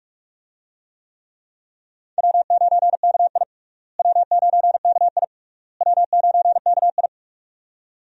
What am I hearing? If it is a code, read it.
Morse code, heard as a keyed tone, about 46 wpm, 700 Hz. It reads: W9KI W9KI W9KI
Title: USA Call Signs - 4 Characters - Repeated 3x - 45wpm